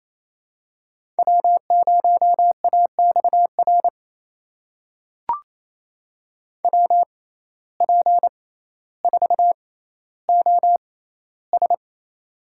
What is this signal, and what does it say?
Morse code, heard as a keyed tone, about 28 wpm, spaced as for 14 wpm, 700 Hz. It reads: W0AXR WP4OS
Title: USA Call Signs - 5 Characters - 6x Character Spacing - 28wpm